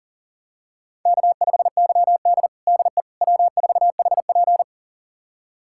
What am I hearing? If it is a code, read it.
Morse code, heard as a keyed tone, about 40 wpm, 700 Hz. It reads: K5YD DE W4HP
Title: USA Callsign DE Callsign - 4 Characters - 40wpm